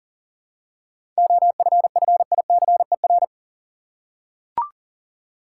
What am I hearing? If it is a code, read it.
Morse code, heard as a keyed tone, about 39 wpm, 700 Hz. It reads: OFFICER
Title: Seven Letter Words - 40wpm